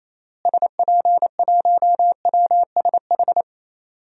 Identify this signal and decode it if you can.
Morse code, heard as a keyed tone, about 28 wpm, 700 Hz. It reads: SP1WSH
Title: DXCC-100 Call Signs - 6 Characters - Repeated 3x - 28wpm